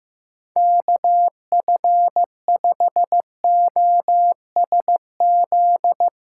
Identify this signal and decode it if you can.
Morse code, heard as a keyed tone, about 15 wpm, 700 Hz. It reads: KF5OSZ
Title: USA Callsign DE Callsign 15wpm